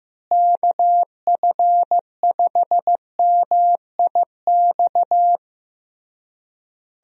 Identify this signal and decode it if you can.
Morse code, heard as a keyed tone, about 15 wpm, 700 Hz. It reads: KF5MIX
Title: USA Call Signs - 6 Characters - Repeated 3x - 15wpm